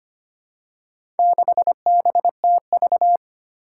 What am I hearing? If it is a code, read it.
Morse code, heard as a keyed tone, about 25 wpm, 700 Hz. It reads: 6BTV
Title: Four Challenging Letter-Number Combinations 25wpm